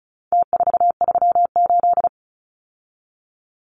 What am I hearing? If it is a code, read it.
Morse code, heard as a keyed tone, about 34 wpm, 700 Hz. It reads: T438
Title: Numbers - Contest Style - 4-Digit 35wpm